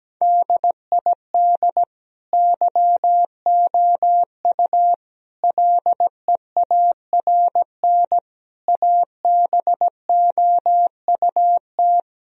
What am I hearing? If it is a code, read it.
Morse code, heard as a keyed tone, about 17 wpm, 700 Hz. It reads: DID YOU LEARN ABOUT
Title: Sentences from Top 1000 Words - Review Part 4 - 17wpm